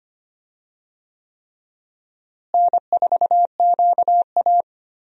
Morse code, heard as a keyed tone, about 25 wpm, 700 Hz. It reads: N4QA
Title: USA Call Signs - 4 Characters - Repeated 3x - 25wpm